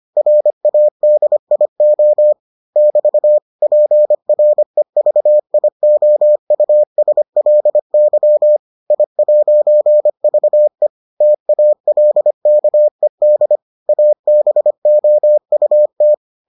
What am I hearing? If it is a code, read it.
Morse code, heard as a keyed tone, about 25 wpm, 600 Hz. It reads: RADIO <BT> PREVIOUSLY I'VE TALKED ABOUT